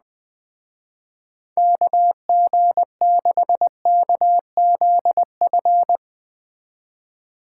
Morse code, heard as a keyed tone, about 20 wpm, 700 Hz. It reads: KG6KZF